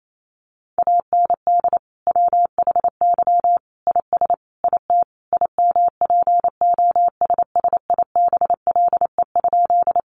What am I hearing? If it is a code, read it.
Morse code, heard as a keyed tone, about 28 wpm, 700 Hz. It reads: AND WHY IS IT IMPOSSIBLE?
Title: Sentences from Top 1100 Words 28wpm